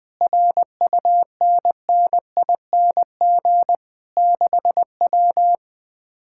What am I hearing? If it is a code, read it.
Morse code, heard as a keyed tone, about 20 wpm, 700 Hz. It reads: RUNNING 6W
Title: Speed-Racing Common QSO Phrases - 17wpm